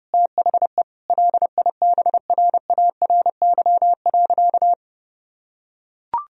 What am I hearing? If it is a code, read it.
Morse code, heard as a keyed tone, about 30 wpm, 700 Hz. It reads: THE LIBRARY.
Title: Sentences from Top 1000 Words - Review Part 4 - 30wpm